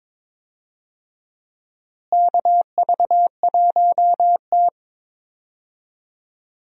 Morse code, heard as a keyed tone, about 22 wpm, 700 Hz. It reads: KV1T
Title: USA Call Signs - 4 Characters - Repeated 3x - 22wpm